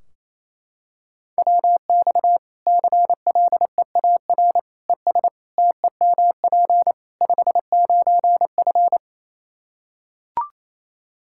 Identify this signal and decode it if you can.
Morse code, heard as a keyed tone, about 28 wpm, 700 Hz. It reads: WX CLEAR ES TEMP 59F
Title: Practice QSO - Exchange 2 - Info - 28wpm